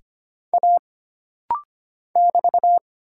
Morse code, heard as a keyed tone, about 25 wpm, 700 Hz. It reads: A <BT>
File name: Single Character (Letter:Number:Punctuation:Prosign) - Rapid-Fire - 25wpm